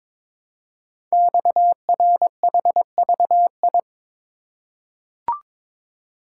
Morse code, heard as a keyed tone, about 23 wpm, 700 Hz. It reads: XRHVI
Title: Code Groups - 5 Letters - 22wpm